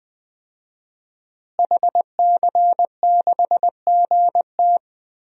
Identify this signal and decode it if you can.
Morse code, heard as a keyed tone, about 20 wpm, 700 Hz. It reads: HC6GT
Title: DXCC-100 Call Signs - 5 Characters - Repeated 3x - 20wpm